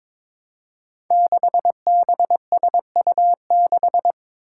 Morse code, heard as a keyed tone, about 22 wpm, 700 Hz. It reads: 6BSU6
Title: Five Challenging Letter-Number Combinations 22wpm